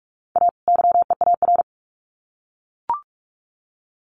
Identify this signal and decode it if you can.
Morse code, heard as a keyed tone, about 44 wpm, 700 Hz. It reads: A YEAR
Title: Sets of 2 Words 45wpm